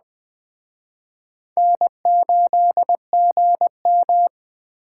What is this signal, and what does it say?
Morse code, heard as a keyed tone, about 20 wpm, 700 Hz. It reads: N8GM
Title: Speed Racing - USA 1x2 Call Signs 17wpm